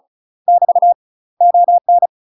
Morse code, heard as a keyed tone, about 37 wpm, 700 Hz. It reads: <BT> ON
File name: UK MASTER QSOs TRAINING / 35WPM / UK MASTER QSOs 0024